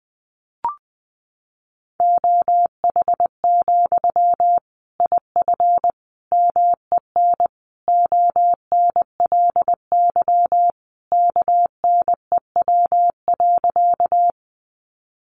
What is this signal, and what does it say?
Morse code, heard as a keyed tone, about 20 wpm, 700 Hz. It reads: OH, IF MEN ONLY KNEW.
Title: Sentences from Top 500 Words 20wpm